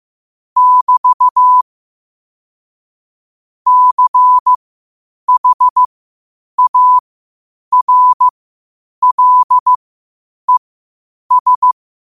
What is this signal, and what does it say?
Morse code, heard as a keyed tone, about 15 wpm, spaced as for 8 wpm, 1000 Hz. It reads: <BT> CHARLES
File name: qotdfiles / 8wpm